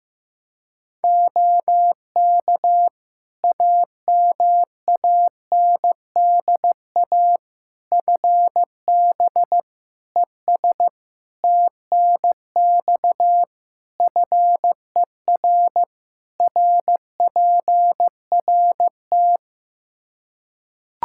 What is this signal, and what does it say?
Morse code, heard as a keyed tone, about 15 wpm, 700 Hz. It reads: OK AMANDA FB ES TNX FER RPRT
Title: K1USN SST QSO Practice 15wpm